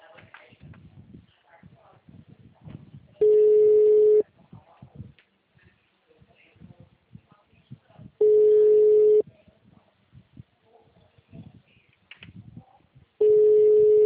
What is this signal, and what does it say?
Ringback tone (Europe, 425 Hz), with speech.
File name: out- -2031- - - .45124